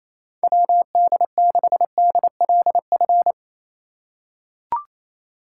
Morse code, heard as a keyed tone, about 28 wpm, 700 Hz. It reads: WD6DLF
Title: USA Call Signs - 4x Character Spacing - 28wpm